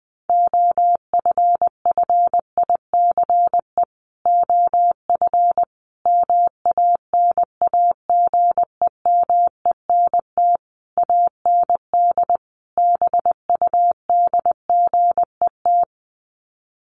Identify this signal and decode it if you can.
Morse code, heard as a keyed tone, about 20 wpm, 700 Hz. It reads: OFFICE OF MANAGEMENT AND BUDGET
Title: Sets of 5 Words Encore2 20wpm